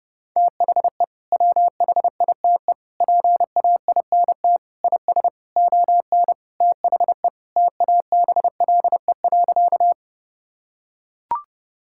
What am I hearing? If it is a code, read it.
Morse code, heard as a keyed tone, about 32 wpm, 700 Hz. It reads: THE WHITE PAINT IS ON THE TABLE.
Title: Sentences from Top 1000 Words - Review Part 6 - 30wpm